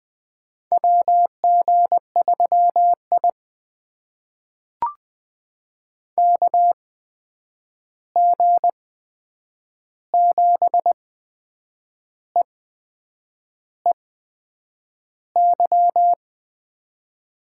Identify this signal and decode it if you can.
Morse code, heard as a keyed tone, about 20 wpm, spaced as for 8 wpm, 700 Hz. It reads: WG3I KG7EEY
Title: USA Call Signs - 8x Character Spacing - 20wpm